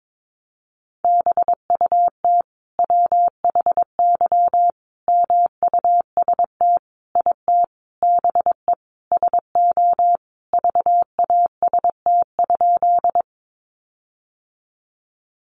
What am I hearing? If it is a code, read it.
Morse code, heard as a keyed tone, about 22 wpm, 700 Hz. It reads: BUT WHY MUST IT BE SO VAST?
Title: Sentences from Top 1300 Words 22wpm